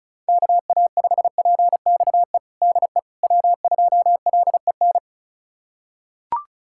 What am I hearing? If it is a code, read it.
Morse code, heard as a keyed tone, about 36 wpm, 700 Hz. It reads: KA5PXE DE W2LEN
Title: USA Callsign DE Callsign 35wpm